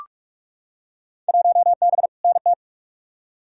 Morse code, heard as a keyed tone, about 43 wpm, 700 Hz. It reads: 1B NT
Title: Field Day Exchanges Encore - 45wpm